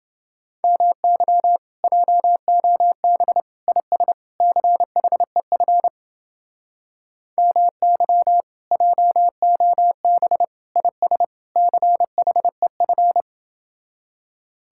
Morse code, heard as a keyed tone, about 30 wpm, 700 Hz. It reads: MY JOB IS CHEF MY JOB IS CHEF